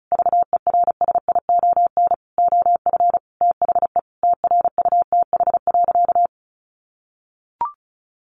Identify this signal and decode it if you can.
Morse code, heard as a keyed tone, about 34 wpm, 700 Hz. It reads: VERSION OF THE TRUTH.
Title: Sentences from Top 1400 Words 35wpm